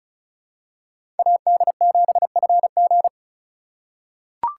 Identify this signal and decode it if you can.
Morse code, heard as a keyed tone, about 34 wpm, 700 Hz. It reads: AD7FG
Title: USA Call Signs - 2x Character Spacing - 35wpm